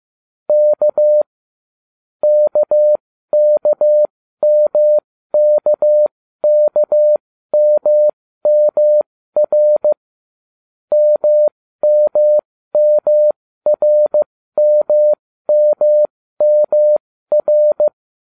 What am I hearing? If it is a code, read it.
Morse code, heard as a keyed tone, about 15 wpm, spaced as for 13 wpm, 600 Hz. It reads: K KKMKKMMR MMMRMMMR